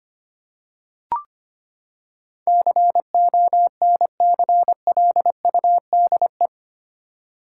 Morse code, heard as a keyed tone, about 25 wpm, 700 Hz. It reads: CONCLUDE
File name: Eight Letter Words - 25wpm